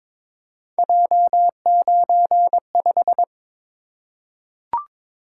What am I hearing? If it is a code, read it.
Morse code, heard as a keyed tone, about 22 wpm, 700 Hz. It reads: J95
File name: practice / 22wpm / Three Letter-Number Combinations - 22wpm